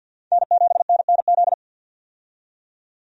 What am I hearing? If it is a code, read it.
Morse code, heard as a keyed tone, about 52 wpm, 700 Hz. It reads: N7NNZ
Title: USA Call Signs - 5 Characters - 50wpm